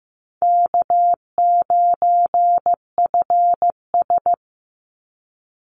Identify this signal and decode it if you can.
Morse code, heard as a keyed tone, about 15 wpm, 700 Hz. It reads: K9FS
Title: Easy Call Signs - 15wpm